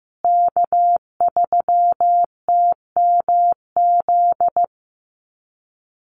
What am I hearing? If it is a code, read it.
Morse code, heard as a keyed tone, about 15 wpm, 700 Hz. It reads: K3TMZ
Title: USA Call Signs Repeated 2x 15wpm